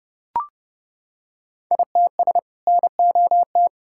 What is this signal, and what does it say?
Morse code, heard as a keyed tone, about 32 wpm, 700 Hz. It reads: ITS NOT
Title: Sentences from Top 1000 Words - Review Part 5 - 30wpm